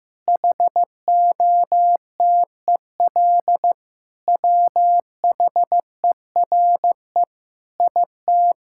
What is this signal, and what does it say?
Morse code, heard as a keyed tone, about 15 wpm, 700 Hz. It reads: HOTEL WHERE IT
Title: Sentences from Top 1600 Words 15wpm